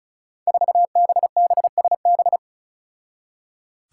Morse code, heard as a keyed tone, about 36 wpm, 700 Hz. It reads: 4BBSB
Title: Five Challenging Letter-Number Combinations 35wpm